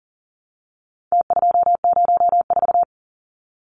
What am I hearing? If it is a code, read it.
Morse code, heard as a keyed tone, about 40 wpm, 700 Hz. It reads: T204
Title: Numbers - Contest Style - 4-Digit 40wpm